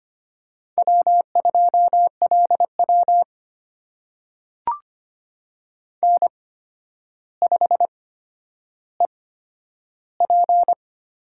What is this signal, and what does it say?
Morse code, heard as a keyed tone, about 25 wpm, spaced as for 11 wpm, 700 Hz. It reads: W2LW N5EP
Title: USA Call Signs - 4 Characters - 8x Character Spacing - 25wpm